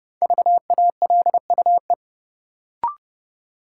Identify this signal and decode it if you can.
Morse code, heard as a keyed tone, about 30 wpm, 700 Hz. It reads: VALUE
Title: Sentences from Top 600 Words - New Word Drill - 30wpm